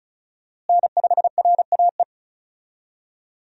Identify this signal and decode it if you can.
Morse code, heard as a keyed tone, about 36 wpm, 700 Hz. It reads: N5RAE
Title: USA Call Signs - 5 Characters - Repeated 3x - 35wpm